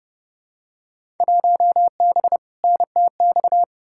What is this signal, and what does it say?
Morse code, heard as a keyed tone, about 31 wpm, 700 Hz. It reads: 1B NTX
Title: Field Day Exchanges - 30wpm